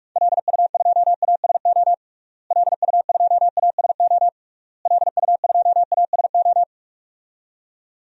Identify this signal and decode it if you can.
Morse code, heard as a keyed tone, about 42 wpm, 700 Hz. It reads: LU2ASO LU2ASO LU2ASO
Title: DXCC-100 Call Signs - 6 Characters - Repeated 3x - 45wpm